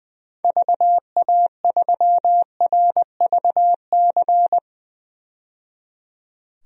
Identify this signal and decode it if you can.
Morse code, heard as a keyed tone, about 21 wpm, 700 Hz. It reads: VA3RVC
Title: DXCC-100 Call Signs - 6 Characters - 20wpm